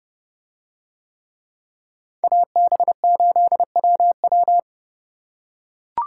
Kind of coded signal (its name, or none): Morse code